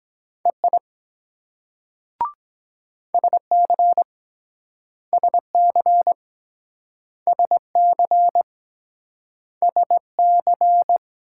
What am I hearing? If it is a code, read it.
Morse code, heard as a keyed tone, about 23 wpm, 700 Hz. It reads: EI SC SC SC SC